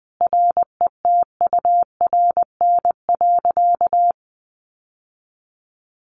Morse code, heard as a keyed tone, about 20 wpm, 700 Hz. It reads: RETURN.
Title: Sentences from Top 1100 Words 20wpm